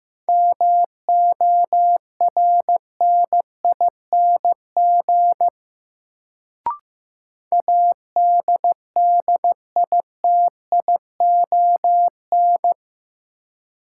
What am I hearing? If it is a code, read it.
Morse code, heard as a keyed tone, about 15 wpm, 700 Hz. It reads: MORNING ADDITION
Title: Sets of 3 Words - New Word Drill - 15wpm